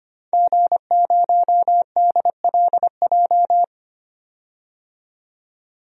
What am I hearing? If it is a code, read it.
Morse code, heard as a keyed tone, about 25 wpm, 700 Hz. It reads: G0DLJ
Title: DXCC-100 Call Signs - 5 Characters - 25wpm